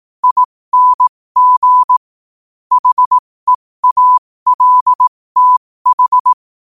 Morse code, heard as a keyed tone, about 18 wpm, spaced as for 15 wpm, 1000 Hz. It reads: ING HEALTH
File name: qotdfiles / 15wpm